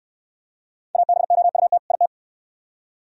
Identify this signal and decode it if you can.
Morse code, heard as a keyed tone, about 64 wpm, 700 Hz. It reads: ABOUT IT